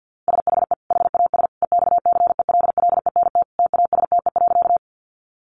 Morse code, heard as a keyed tone, about 48 wpm, 700 Hz. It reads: SHE HAS EXCELLENT TASTE.
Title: Sentences from Top 1100 Words 50wpm